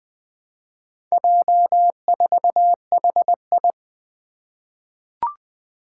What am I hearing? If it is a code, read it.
Morse code, heard as a keyed tone, about 20 wpm, 700 Hz. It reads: J4HI
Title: Four Challenging Letter-Number Combinations 20wpm